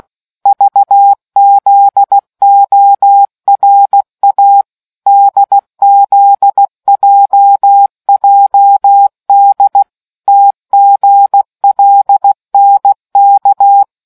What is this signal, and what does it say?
Morse code, heard as a keyed tone, about 16 wpm, 800 Hz. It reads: VZORA DZJJD TGLNK